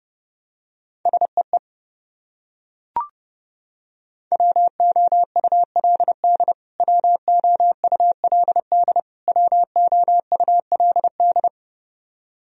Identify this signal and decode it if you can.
Morse code, heard as a keyed tone, about 30 wpm, 700 Hz. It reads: SEE WOULD WOULD WOULD
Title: Top 100 Words Repeated 30wpm